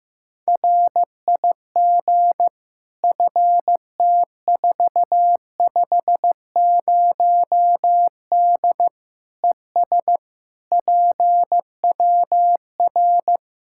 Morse code, heard as a keyed tone, about 15 wpm, 700 Hz. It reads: RIG FT450D ES PWR
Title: Practice QSO - Exchange 2 - Info - 15wpm